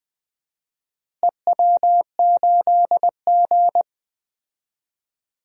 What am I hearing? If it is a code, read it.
Morse code, heard as a keyed tone, about 20 wpm, 700 Hz. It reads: EW8G
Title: DXCC-100 Call Signs - 4 Characters - Repeated 3x - 20wpm